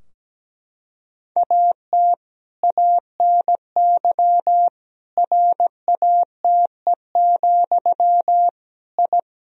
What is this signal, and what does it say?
Morse code, heard as a keyed tone, about 17 wpm, 700 Hz. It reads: AT ANY RATE, I
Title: Sentences from Top 1000 Words - Review Part 4 - 17wpm